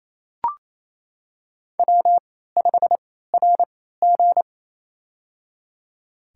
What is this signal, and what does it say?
Morse code, heard as a keyed tone, about 28 wpm, spaced as for 16 wpm, 700 Hz. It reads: W5RG
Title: USA Call Signs - 4 Characters - 3x Character Spacing - 28wpm